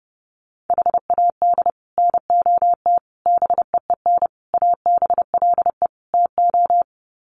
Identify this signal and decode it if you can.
Morse code, heard as a keyed tone, about 30 wpm, 700 Hz. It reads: HAD NOT BEEN ABLE TO
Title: Sets of 5 Words Encore 30wpm